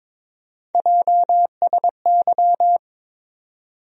Morse code, heard as a keyed tone, about 22 wpm, 700 Hz. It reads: JSY
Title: USA Call Sign Postfixes - 3 Letters 22wpm